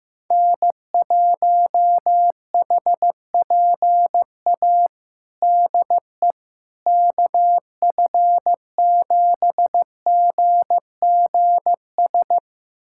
Morse code, heard as a keyed tone, about 16 wpm, 700 Hz. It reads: N1HPA DE KF7GGS
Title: USA Callsign DE Callsign 15wpm